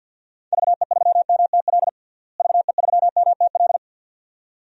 Morse code, heard as a keyed tone, about 52 wpm, 700 Hz. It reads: VE3MTL VE3MTL
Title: DXCC-100 Call Signs - 6 Characters - Repeated 2x - 50wpm